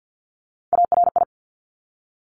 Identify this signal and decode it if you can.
Morse code, heard as a keyed tone, about 50 wpm, 700 Hz. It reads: ARI
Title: Top Three Letter Combinations in Words 50wpm